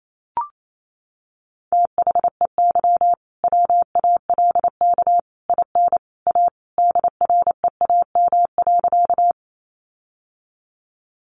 Morse code, heard as a keyed tone, about 28 wpm, 700 Hz. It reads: THEY WALK IN A DREAM.
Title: Sentences from Top 1500 Words 28wpm